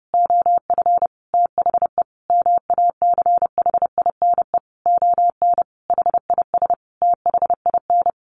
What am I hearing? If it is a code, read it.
Morse code, heard as a keyed tone, about 28 wpm, 700 Hz. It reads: OF THE MACHINE ON HIS THIN
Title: A Sound Of Thunder 30wpm